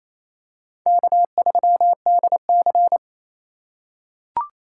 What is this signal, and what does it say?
Morse code, heard as a keyed tone, about 28 wpm, 700 Hz. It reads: K3DC
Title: USA Call Signs - 4x Character Spacing - 28wpm